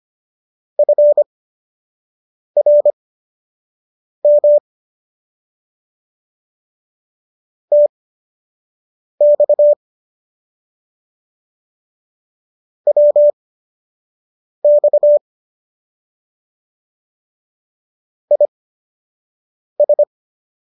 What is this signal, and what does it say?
Morse code, heard as a keyed tone, about 25 wpm, spaced as for 6 wpm, 590 Hz. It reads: FRM TX WX IS